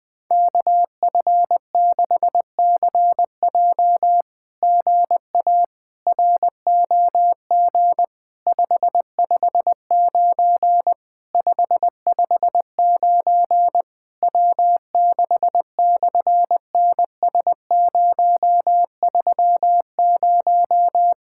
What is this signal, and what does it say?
Morse code, heard as a keyed tone, about 20 wpm, 700 Hz. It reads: KF6CJ GA ROG 559 559 W6/NS030
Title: SOTA QSO Practice 20wpm